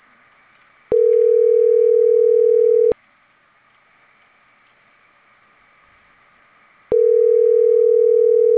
Ringback tone (North America), 440 + 480 Hz.